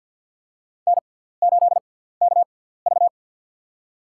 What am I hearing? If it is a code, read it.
Morse code, heard as a keyed tone, about 53 wpm, spaced as for 17 wpm, 690 Hz. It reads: N8KV
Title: USA Call Signs - 4 Characters - 6x Character Spacing - 50wpm